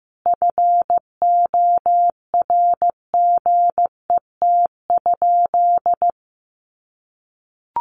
Morse code, heard as a keyed tone, about 15 wpm, 700 Hz. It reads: FORGET?
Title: Sentences from Top 1300 Words 15wpm